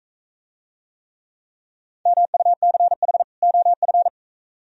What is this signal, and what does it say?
Morse code, heard as a keyed tone, about 42 wpm, 700 Hz. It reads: MUCH OF